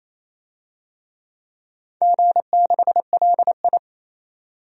Morse code, heard as a keyed tone, about 27 wpm, 690 Hz. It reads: G6LI
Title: DXCC-100 Call Signs - 4 Characters - 28wpm